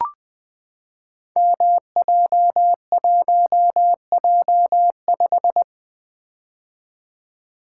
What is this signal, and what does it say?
Morse code, heard as a keyed tone, about 20 wpm, 700 Hz. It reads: MJ1J5
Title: Code Groups - 5 Letter:Numbers - 20wpm